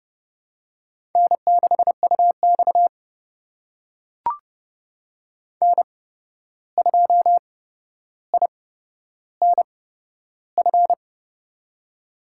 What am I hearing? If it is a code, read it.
Morse code, heard as a keyed tone, about 30 wpm, spaced as for 12 wpm, 700 Hz. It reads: N6UX N2INF
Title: USA Call Signs - 8x Character Spacing - 30wpm